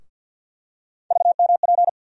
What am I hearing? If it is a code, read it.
Morse code, heard as a keyed tone, about 50 wpm, 690 Hz. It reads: VMP
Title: USA Call Sign Postfixes - 3 Letters 50wpm